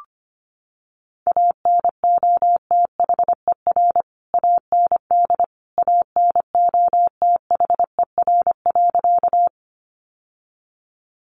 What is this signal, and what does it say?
Morse code, heard as a keyed tone, about 25 wpm, 700 Hz. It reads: ANOTHER AND ANOTHER.
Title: Sentences from Top 600 Words 25wpm